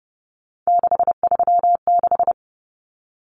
Morse code, heard as a keyed tone, about 30 wpm, 700 Hz. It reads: 636
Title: Common Cut Numbers - 3-Digit 30wpm